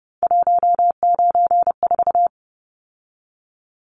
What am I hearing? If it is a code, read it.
Morse code, heard as a keyed tone, about 30 wpm, 700 Hz. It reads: 194
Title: Numbers - 3-Digit 30wpm